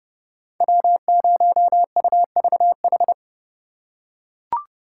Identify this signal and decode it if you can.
Morse code, heard as a keyed tone, about 29 wpm, 700 Hz. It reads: W0UVH